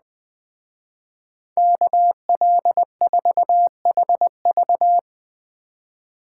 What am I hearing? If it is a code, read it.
Morse code, heard as a keyed tone, about 19 wpm, 700 Hz. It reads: KL4HV
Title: Speed Racing - USA 2x2 Call Signs 17wpm